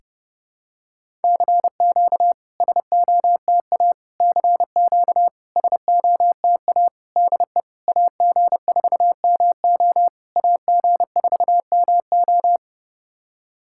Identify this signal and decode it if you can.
Morse code, heard as a keyed tone, about 30 wpm, 690 Hz. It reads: CQ SOTA CQ SOTA DE AG4MO AG4MO
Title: SOTA QSO Practice 30wpm